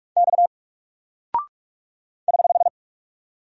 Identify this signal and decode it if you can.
Morse code, heard as a keyed tone, about 44 wpm, 700 Hz. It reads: X <HH>
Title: Single Character (Letter:Number:Punctuation:Prosign) Alt 45wpm